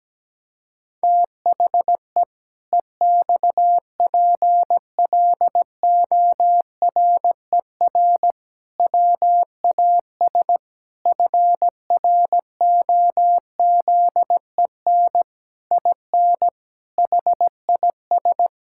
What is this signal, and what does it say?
Morse code, heard as a keyed tone, about 17 wpm, 700 Hz. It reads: THE EXPLORER WAS FROZEN IN HIS
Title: Pangram Challenge v2 - 17wpm